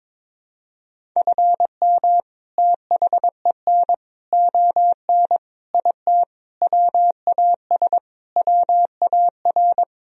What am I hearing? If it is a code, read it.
Morse code, heard as a keyed tone, about 22 wpm, 700 Hz. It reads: FM THEN ON IT WAS WAR
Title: Sentences from Top 400 Words and Abbreviations 22wpm